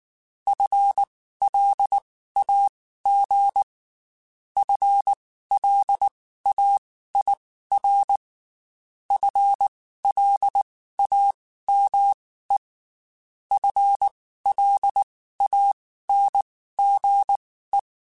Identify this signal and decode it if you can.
Morse code, heard as a keyed tone, about 19 wpm, spaced as for 14 wpm, 800 Hz. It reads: FLAG FLAIR FLAME FLANGE
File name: Spaced / FL Runs sp mp3s / FL Runs sp @18